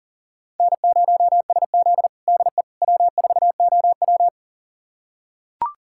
Morse code, heard as a keyed tone, about 40 wpm, 700 Hz. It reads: N0SZ DE W4OW
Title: USA Callsign DE Callsign - 4 Characters - 40wpm